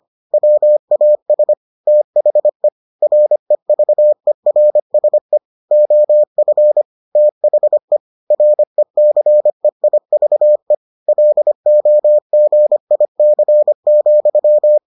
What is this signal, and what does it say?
Morse code, heard as a keyed tone, about 25 wpm, 600 Hz. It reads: WAS THE REVERSE OF THE RECEIVE LOGIC,